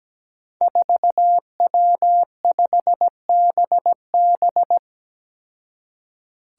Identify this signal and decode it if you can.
Morse code, heard as a keyed tone, about 17 wpm, 700 Hz. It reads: 4W5BB
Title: Five Challenging Letter-Number Combinations 17wpm